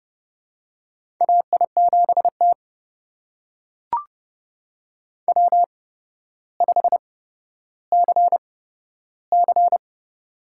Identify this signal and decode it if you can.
Morse code, heard as a keyed tone, about 31 wpm, spaced as for 13 wpm, 700 Hz. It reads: AI7T W5CC
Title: USA Call Signs - 4 Characters - 8x Character Spacing - 30wpm